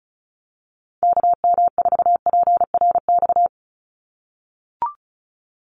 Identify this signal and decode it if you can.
Morse code, heard as a keyed tone, about 34 wpm, 700 Hz. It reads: KM4PRX
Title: USA Call Signs 35wpm